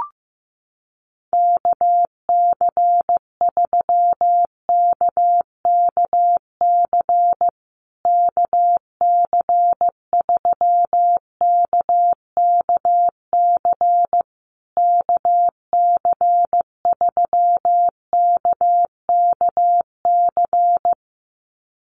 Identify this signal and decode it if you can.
Morse code, heard as a keyed tone, about 15 wpm, 700 Hz. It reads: KC3KKC KC3KKC KC3KKC
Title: USA Call Signs Repeated 3x 15wpm